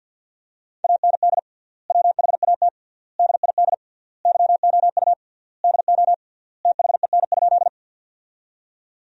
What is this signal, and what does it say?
Morse code, heard as a keyed tone, about 50 wpm, 700 Hz. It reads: AND WHAT DID YOU DO THEN?
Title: 3x Word-Spacing - Sentences from Top 100 Words - 50wpm